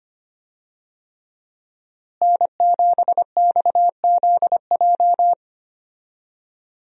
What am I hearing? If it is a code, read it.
Morse code, heard as a keyed tone, about 25 wpm, 700 Hz. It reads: N7XZJ
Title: USA Call Signs - 5 Characters - Repeated 3x - 25wpm